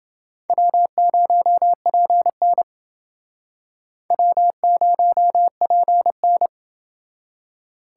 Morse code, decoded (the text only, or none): W0PN W0PN